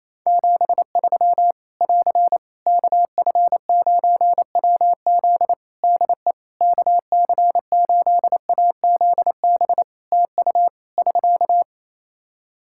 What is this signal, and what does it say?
Morse code, heard as a keyed tone, about 27 wpm, 700 Hz. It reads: 73 <AR> KF9WZ DE KC8AZB TU <SK>
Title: Practice QSO - Exchange 4 - Ending - 28wpm